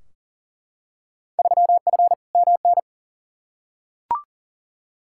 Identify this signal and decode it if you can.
Morse code, heard as a keyed tone, about 40 wpm, 700 Hz. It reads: 3F MN
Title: Field Day Exchanges Encore - 40wpm